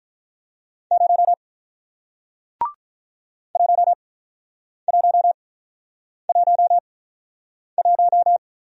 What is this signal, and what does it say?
Morse code, heard as a keyed tone, about 48 wpm, spaced as for 9 wpm, 700 Hz. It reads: 0 1111